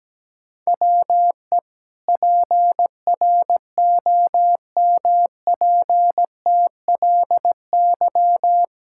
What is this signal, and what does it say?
Morse code, heard as a keyed tone, about 17 wpm, 700 Hz. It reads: WE PROMPTLY
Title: Pangram Challenge v2 - 17wpm